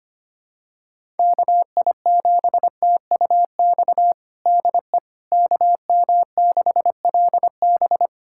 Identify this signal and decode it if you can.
Morse code, heard as a keyed tone, about 25 wpm, 700 Hz. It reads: KI7TUX DE KM6LB
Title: USA Callsign DE Callsign 25wpm